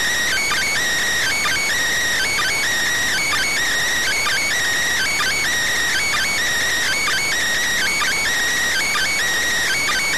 Emergency Alert System (SAME) tones (header bursts), heard as harsh buzzing bursts. It is heard from 0.2 to 9.1 s.